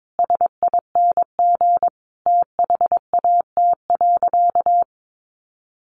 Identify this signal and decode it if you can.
Morse code, heard as a keyed tone, about 22 wpm, 700 Hz. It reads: SING THAT.
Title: Sentences from Top 500 Words 22wpm